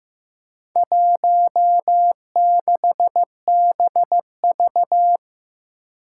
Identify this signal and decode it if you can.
Morse code, heard as a keyed tone, about 15 wpm, 700 Hz. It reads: 16BV
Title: Four Challenging Letter-Number Combinations 15wpm